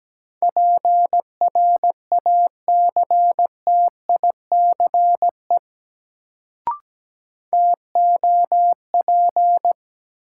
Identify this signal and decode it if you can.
Morse code, heard as a keyed tone, about 17 wpm, 700 Hz. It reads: PRACTICE TOP
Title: Sentences from Top 700 Words - New Word Drill - 17wpm